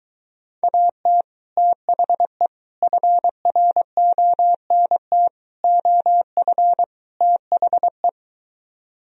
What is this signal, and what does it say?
Morse code, heard as a keyed tone, about 23 wpm, 700 Hz. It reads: AT THE FRONT OF THE